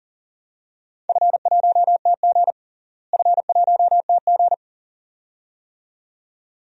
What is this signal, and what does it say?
Morse code, heard as a keyed tone, about 40 wpm, 700 Hz. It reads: F1TG F1TG
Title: DXCC-100 Call Signs - 4 Characters - Repeated 2x - 40wpm